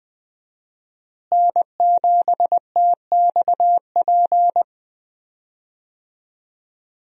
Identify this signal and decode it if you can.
Morse code, heard as a keyed tone, about 20 wpm, 700 Hz. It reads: N7TXP
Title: USA Call Signs - 5 Characters - Repeated 2x - 20wpm